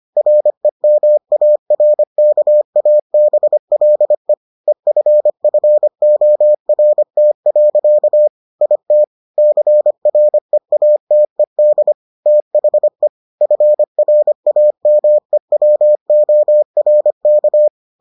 Morse code, heard as a keyed tone, about 25 wpm, 600 Hz. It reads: REMARKABLE EFFORT. IT CREATED THE FRAMEWORK